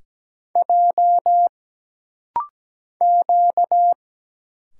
Morse code, heard as a keyed tone, about 17 wpm, 700 Hz. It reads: J Q
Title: Single Letters - Rapid-Fire 17wpm